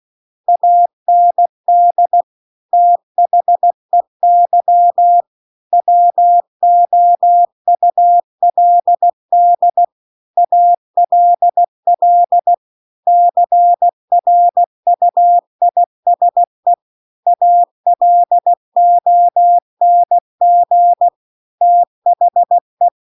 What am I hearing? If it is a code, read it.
Morse code, heard as a keyed tone, about 16 wpm, 700 Hz. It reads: AND THEY WOULD ALL CRUISE ALONG THE